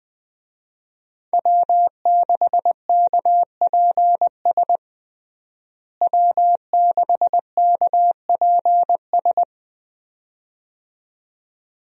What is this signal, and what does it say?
Morse code, heard as a keyed tone, about 20 wpm, 700 Hz. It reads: W6KPS W6KPS
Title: USA Call Signs - 5 Characters - Repeated 2x - 20wpm